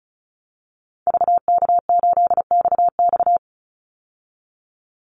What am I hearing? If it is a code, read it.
Morse code, heard as a keyed tone, about 34 wpm, 700 Hz. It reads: VK8XX
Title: Call Signs Repeated 2x - 35wpm